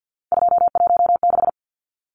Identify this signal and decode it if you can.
Morse code, heard as a keyed tone, about 50 wpm, 700 Hz. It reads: 216
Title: Numbers - 3-Digit 50wpm